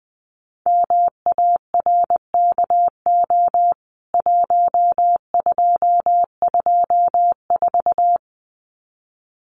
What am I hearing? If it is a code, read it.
Morse code, heard as a keyed tone, about 20 wpm, 700 Hz. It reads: MARKO 1224
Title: CWT Exchange 20wpm